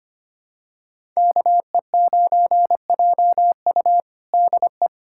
Morse code, heard as a keyed tone, about 25 wpm, 700 Hz. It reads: KE9JU DE
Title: USA Callsign DE Callsign 25wpm